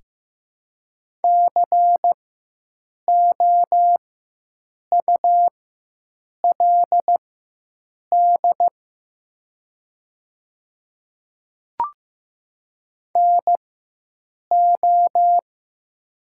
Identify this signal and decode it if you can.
Morse code, heard as a keyed tone, about 15 wpm, spaced as for 7 wpm, 700 Hz. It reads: COULD NO
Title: 4x Character Spacing - Top 100 Words - 15wpm